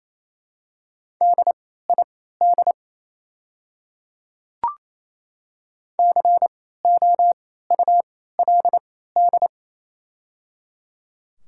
Morse code, heard as a keyed tone, about 28 wpm, spaced as for 16 wpm, 700 Hz. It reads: DID COULD
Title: 3x Character Spacing - Top 100 Words - 28wpm